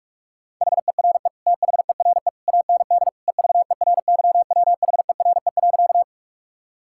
Morse code, heard as a keyed tone, about 46 wpm, 700 Hz. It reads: HERE THERE AND EVERYWHERE.
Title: Sentences from Top 1600 Words 45wpm